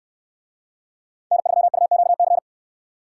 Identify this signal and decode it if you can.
Morse code, heard as a keyed tone, about 64 wpm, 700 Hz. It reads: N3UQC